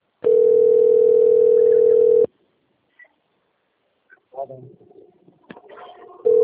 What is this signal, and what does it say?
Ringback tone (North America), 440 + 480 Hz, with speech.
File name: rg-1600-unknown- - - .40815